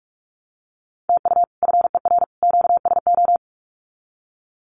Morse code, heard as a keyed tone, about 44 wpm, 700 Hz. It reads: TU FER QSO
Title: Common QSO Phrases 45wpm